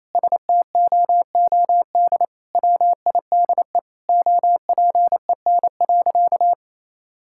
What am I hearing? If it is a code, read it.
Morse code, heard as a keyed tone, about 28 wpm, 700 Hz. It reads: STOOD WIDE OPEN.
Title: Sentences from Top 1000 Words - Review Part 6 - 28wpm